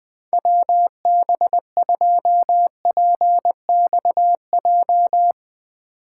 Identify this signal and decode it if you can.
Morse code, heard as a keyed tone, about 21 wpm, 700 Hz. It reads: WB2PXJ
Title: Speed Racing - USA 2x3 Call Signs 20wpm